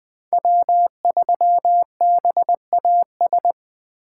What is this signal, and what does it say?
Morse code, heard as a keyed tone, about 20 wpm, 700 Hz. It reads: W3BAS
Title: Speed Racing - USA 1x3 Call Signs 17wpm